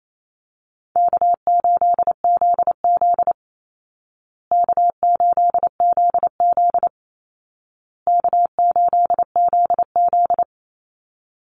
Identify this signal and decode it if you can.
Morse code, heard as a keyed tone, about 28 wpm, 700 Hz. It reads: K8ZZ K8ZZ K8ZZ
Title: Easy Call Signs Repeated 3x - 28wpm